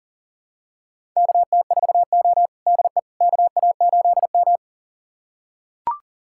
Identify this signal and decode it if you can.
Morse code, heard as a keyed tone, about 40 wpm, 690 Hz. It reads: KT4O DE KA8M
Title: USA Callsign DE Callsign - 4 Characters - 40wpm